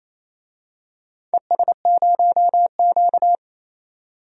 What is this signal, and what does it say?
Morse code, heard as a keyed tone, about 27 wpm, 690 Hz. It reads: ES0Q
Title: DXCC-100 Call Signs - 4 Characters - 28wpm